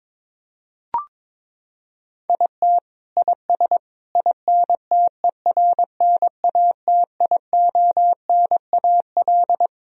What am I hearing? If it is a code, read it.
Morse code, heard as a keyed tone, about 22 wpm, 700 Hz. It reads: IT IS INTERNATIONAL
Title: Sentences from Top 1000 Words - Review Part 4 - 22wpm